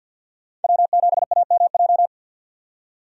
Morse code, heard as a keyed tone, about 48 wpm, 700 Hz. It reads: W7AMJ
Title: USA Call Signs - 5 Characters - 50wpm